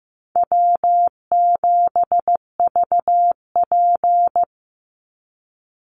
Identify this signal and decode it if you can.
Morse code, heard as a keyed tone, about 14 wpm, 700 Hz. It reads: W7VP
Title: Easy Call Signs Repeated 3x - 15wpm